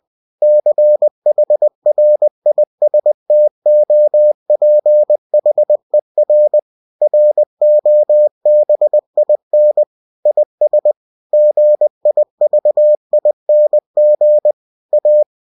Morse code, heard as a keyed tone, about 20 wpm, 600 Hz. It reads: CHRISTOPHER ROBIN IS GIVING A
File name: cw-books / winnie-the-pooh / winnie-20wpm-page-0102